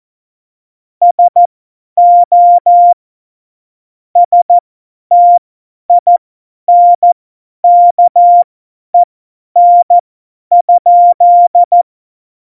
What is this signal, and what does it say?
Morse code, heard as a keyed tone, about 14 wpm, spaced as for 10 wpm, 700 Hz. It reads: SO STINKEN?
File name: Morsekurs / Witze / Witz 28. Mai 2019, 50 F70